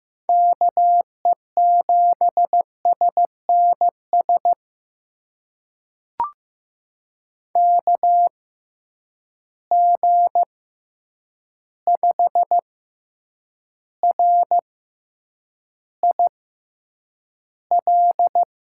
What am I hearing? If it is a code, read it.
Morse code, heard as a keyed tone, about 15 wpm, spaced as for 8 wpm, 700 Hz. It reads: KE7SNS KG5RIL
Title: USA Call Signs - 6 Characters - 6x Character Spacing - 15wpm